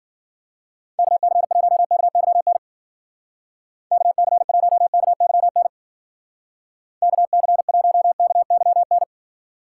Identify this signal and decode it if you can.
Morse code, heard as a keyed tone, about 53 wpm, 700 Hz. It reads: KC1KYN KC1KYN KC1KYN